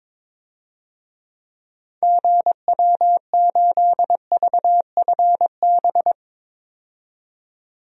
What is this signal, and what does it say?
Morse code, heard as a keyed tone, about 22 wpm, 700 Hz. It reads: GW8VFB
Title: DXCC-100 Call Signs - 6 Characters - 22wpm